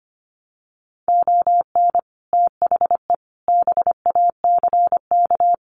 Morse code, heard as a keyed tone, about 25 wpm, 690 Hz. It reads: ON THE BACK
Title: Sets of 5 Words Encore 25wpm